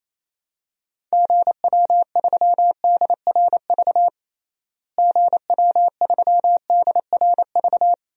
Morse code, heard as a keyed tone, about 28 wpm, 700 Hz. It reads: GW3DRV GW3DRV
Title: DXCC-100 Call Signs - 6 Characters - Repeated 3x - 28wpm